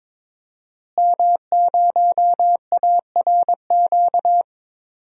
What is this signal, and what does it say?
Morse code, heard as a keyed tone, about 23 wpm, 700 Hz. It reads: M0ARQ
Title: DXCC-100 Call Signs - 5 Characters - Repeated 2x - 22wpm